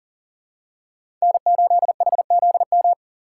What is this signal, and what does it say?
Morse code, heard as a keyed tone, about 39 wpm, 700 Hz. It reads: N8HZM
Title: K1USN SST QSO Practice - 40wpm